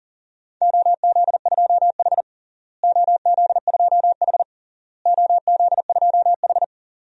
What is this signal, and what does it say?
Morse code, heard as a keyed tone, about 37 wpm, 700 Hz. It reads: OZ2H OZ2H OZ2H
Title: DXCC-100 Call Signs - 4 Characters - Repeated 3x - 40wpm